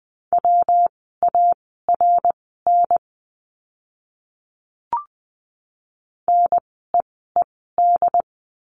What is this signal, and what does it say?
Morse code, heard as a keyed tone, about 20 wpm, spaced as for 15 wpm, 700 Hz. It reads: WARN NEED
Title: 115 Lesson 19f - Letter w - Words 20wpm